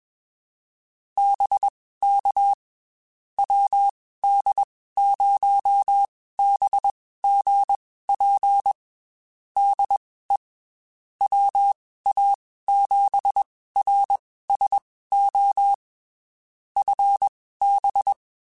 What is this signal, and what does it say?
Morse code, heard as a keyed tone, about 21 wpm, spaced as for 15 wpm, 800 Hz. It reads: BK WD0BGP DE WA7RSO FB
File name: QSOs / QSOs-mp3s / WD0BGP 1@20